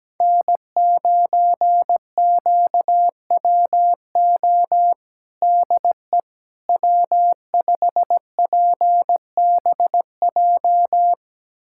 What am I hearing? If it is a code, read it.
Morse code, heard as a keyed tone, about 17 wpm, 700 Hz. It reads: N9QWO DE W5PBJ
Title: USA Callsign DE Callsign - 5 Characters - 17wpm